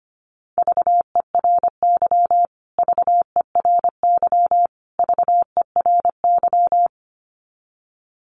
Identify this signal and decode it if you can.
Morse code, heard as a keyed tone, about 25 wpm, 700 Hz. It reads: VERY VERY VERY
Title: Top 500 Words Repeated 25wpm